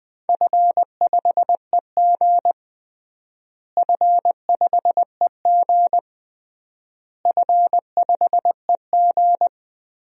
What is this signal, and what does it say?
Morse code, heard as a keyed tone, about 20 wpm, 700 Hz. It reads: F5EG F5EG F5EG
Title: DXCC-100 Call Signs - 4 Characters - Repeated 3x - 20wpm